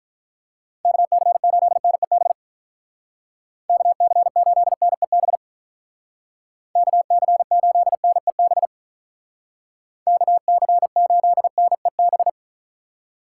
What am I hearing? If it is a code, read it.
Morse code, heard as a keyed tone, about 44 wpm, 700 Hz. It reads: KC8NEB KC8NEB KC8NEB KC8NEB